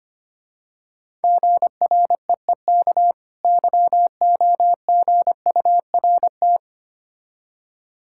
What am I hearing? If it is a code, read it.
Morse code, heard as a keyed tone, about 25 wpm, 700 Hz. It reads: GREEK YOGURT
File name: practice / 25wpm / Difficult Words with Hints - 25wpm